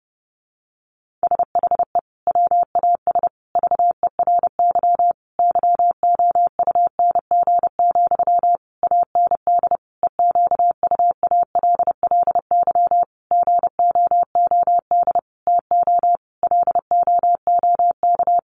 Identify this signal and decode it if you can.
Morse code, heard as a keyed tone, about 30 wpm, 700 Hz. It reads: SHE WAS VERY YOUNG, AND EQUALLY GOOD TO LOOK
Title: Sentences from Top 1400 Words 30wpm